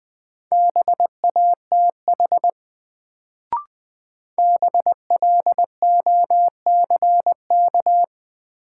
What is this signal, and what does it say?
Morse code, heard as a keyed tone, about 20 wpm, 700 Hz. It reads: BATH BLOCK
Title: Sentences from Top 1600 Words - New Word Drill Alt - 20wpm